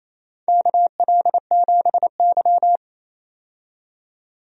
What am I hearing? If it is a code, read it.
Morse code, heard as a keyed tone, about 27 wpm, 700 Hz. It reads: KL7Y
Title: DXCC-100 Call Signs - 4 Characters - 28wpm